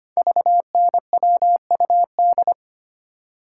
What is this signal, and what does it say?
Morse code, heard as a keyed tone, about 25 wpm, 700 Hz. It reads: VNWUD
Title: Code Groups - 5 Letters - 25wpm